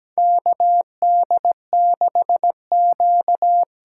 Morse code, heard as a keyed tone, about 17 wpm, 700 Hz. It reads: KD6Q
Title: USA Call Signs - 6 Characters - 8x Character Spacing - 17wpm